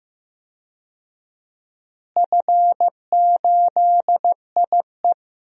Morse code, heard as a keyed tone, about 15 wpm, 700 Hz. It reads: F8IE
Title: DXCC-100 Call Signs - 4 Characters - Repeated 2x - 15wpm